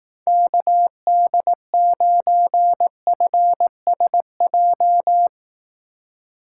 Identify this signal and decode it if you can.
Morse code, heard as a keyed tone, about 18 wpm, 700 Hz. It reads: KD9FSJ
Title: Speed Racing - USA 2x3 Call Signs 15wpm